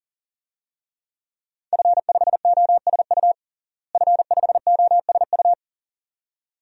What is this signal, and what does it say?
Morse code, heard as a keyed tone, about 40 wpm, 700 Hz. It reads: F5OSU F5OSU
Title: DXCC-100 Call Signs - 5 Characters - Repeated 2x - 40wpm